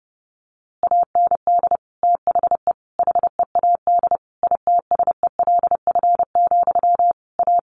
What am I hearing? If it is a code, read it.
Morse code, heard as a keyed tone, about 30 wpm, 700 Hz. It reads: AND THE HEAD ITSELF, A
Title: A Sound Of Thunder 30wpm